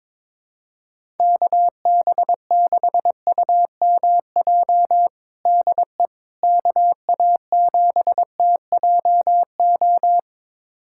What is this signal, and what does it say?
Morse code, heard as a keyed tone, about 22 wpm, 700 Hz. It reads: KB6UMJ DE KA7TJO
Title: USA Callsign DE Callsign - 6 Characters - 22wpm